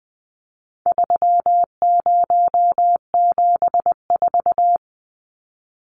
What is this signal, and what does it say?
Morse code, heard as a keyed tone, about 20 wpm, 700 Hz. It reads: 3074
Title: Numbers - 4-Digit 20wpm